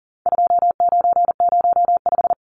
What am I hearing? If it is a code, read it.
Morse code, heard as a keyed tone, about 40 wpm, 700 Hz. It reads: 2905
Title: Numbers - 4-Digit 40wpm